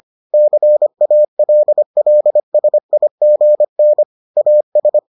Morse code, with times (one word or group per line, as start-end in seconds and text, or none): 0.3-4.0 CALLSIGN
4.4-5.0 AS